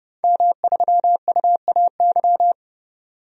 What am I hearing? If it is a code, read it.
Morse code, heard as a keyed tone, about 30 wpm, 700 Hz. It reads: M3UAY
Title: DXCC-100 Call Signs - 5 Characters - 30wpm